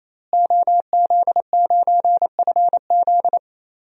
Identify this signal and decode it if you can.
Morse code, heard as a keyed tone, about 28 wpm, 700 Hz. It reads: OZ9FZ
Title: DXCC-100 Call Signs - 5 Characters - Repeated 3x - 28wpm